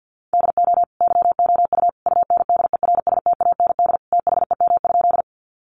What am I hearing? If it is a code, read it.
Morse code, heard as a keyed tone, about 50 wpm, 700 Hz. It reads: DO YOU UNDERSTAND THEM?
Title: Sentences from Top 600 Words 50wpm